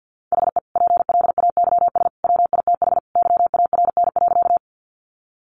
Morse code, heard as a keyed tone, about 50 wpm, 700 Hz. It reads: HE PLAYS WITH YARN.
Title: Easy Phrases 50wpm